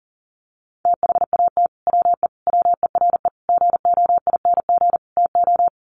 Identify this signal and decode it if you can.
Morse code, heard as a keyed tone, about 40 wpm, 700 Hz. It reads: THAT WE WERE GOING TO
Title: Sets of 5 Words Encore2 40wpm